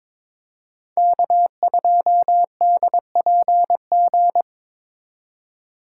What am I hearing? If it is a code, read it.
Morse code, heard as a keyed tone, about 26 wpm, 700 Hz. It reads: K2DPG
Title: Speed Racing - USA Call Signs 22wpm